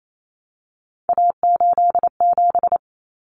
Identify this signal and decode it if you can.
Morse code, heard as a keyed tone, about 28 wpm, 700 Hz. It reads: A87
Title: Common Cut Numbers - 3-Digit 28wpm